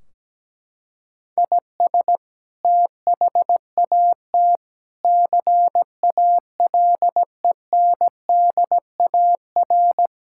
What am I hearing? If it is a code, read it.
Morse code, heard as a keyed tone, about 17 wpm, 700 Hz. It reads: IS THAT CALENDAR
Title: Sentences from Top 1600 Words 17wpm